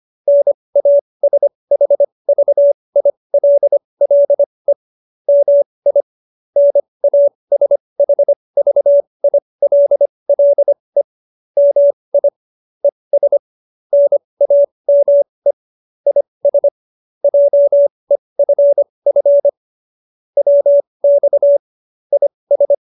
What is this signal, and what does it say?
Morse code, heard as a keyed tone, about 25 wpm, spaced as for 20 wpm, 570 Hz. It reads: NASHVILLE MI NASHVILLE MI ES NAME IS JEFF WX IS